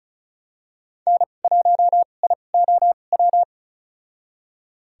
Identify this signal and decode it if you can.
Morse code, heard as a keyed tone, about 36 wpm, spaced as for 25 wpm, 700 Hz. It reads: N1IOW
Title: USA Call Signs - 5 Characters - 2x Character Spacing - 35wpm